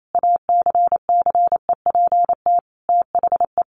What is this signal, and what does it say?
Morse code, heard as a keyed tone, about 28 wpm, 700 Hz. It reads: ACCEPT THE
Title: Sentences from Top 1300 Words 28wpm